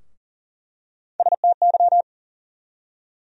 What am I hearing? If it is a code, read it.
Morse code, heard as a keyed tone, about 38 wpm, 700 Hz. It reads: STY
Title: USA Call Sign Postfixes - 3 Letters 40wpm